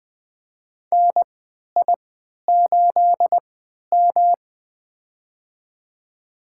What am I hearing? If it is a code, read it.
Morse code, heard as a keyed tone, about 20 wpm, spaced as for 11 wpm, 700 Hz. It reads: NI8M
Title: USA Call Signs - 4 Characters - 3x Character Spacing - 20wpm